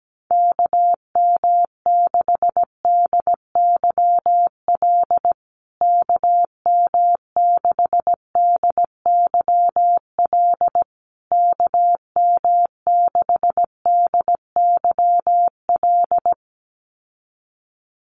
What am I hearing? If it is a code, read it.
Morse code, heard as a keyed tone, about 17 wpm, 700 Hz. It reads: KM6DYL KM6DYL KM6DYL
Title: USA Call Signs Repeated 3x 17wpm